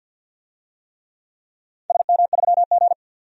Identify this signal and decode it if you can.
Morse code, heard as a keyed tone, about 46 wpm, 700 Hz. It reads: SM3G
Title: DXCC-100 Call Signs - 4 Characters - 50wpm